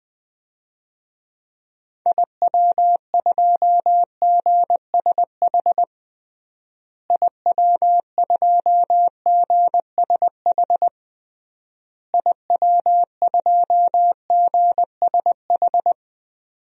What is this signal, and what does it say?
Morse code, heard as a keyed tone, about 20 wpm, 700 Hz. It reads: IW2GSH IW2GSH IW2GSH
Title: DXCC-100 Call Signs - 6 Characters - Repeated 3x - 20wpm